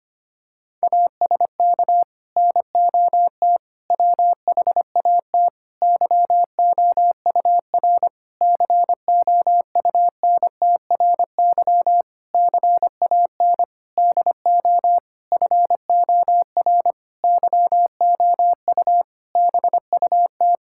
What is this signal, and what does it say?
Morse code, heard as a keyed tone, about 25 wpm, 700 Hz. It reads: ASK NOT WHAT YOUR COUNTRY CAN DO FOR YOU BUT
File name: practice / 25wpm / Sentences with 9-Word Phrases - 25wpm